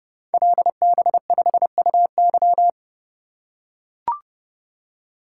Morse code, heard as a keyed tone, about 28 wpm, 700 Hz. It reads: LB5UY
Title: Code Groups - 5 Letter:Numbers - 30wpm